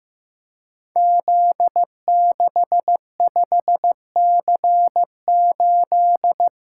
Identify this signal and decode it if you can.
Morse code, heard as a keyed tone, about 15 wpm, 700 Hz. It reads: Z65C8
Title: Code Groups - 5 Letter:Numbers - 15wpm